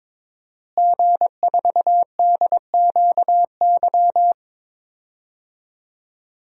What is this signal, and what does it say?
Morse code, heard as a keyed tone, about 22 wpm, 700 Hz. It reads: G4DQY
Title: DXCC-100 Call Signs - 5 Characters - Repeated 2x - 22wpm